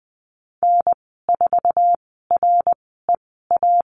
Morse code, heard as a keyed tone, about 20 wpm, spaced as for 14 wpm, 700 Hz. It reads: N4REA
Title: 069 Lesson 12g - Letter d - Call Signs 20wpm